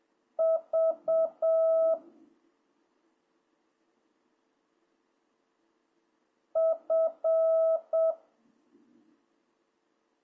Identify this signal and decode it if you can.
Morse code, heard as a keyed tone, about 7 wpm, 650 Hz. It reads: V F